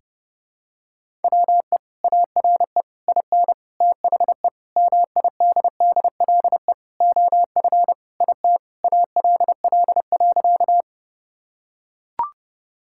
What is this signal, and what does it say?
Morse code, heard as a keyed tone, about 30 wpm, 700 Hz. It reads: WE ARE IN THE MIDDLE OF IT ALL.
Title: Sentences from Top 1000 Words - Review Part 6 - 30wpm